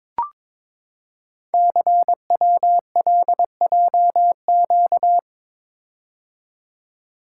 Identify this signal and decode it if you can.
Morse code, heard as a keyed tone, about 22 wpm, 700 Hz. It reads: CWLJQ
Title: Code Groups - 5 Letters - 22wpm